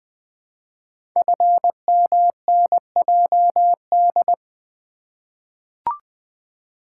Morse code, heard as a keyed tone, about 20 wpm, 700 Hz. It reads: FMNJD
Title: Code Groups - 5 Letters - 20wpm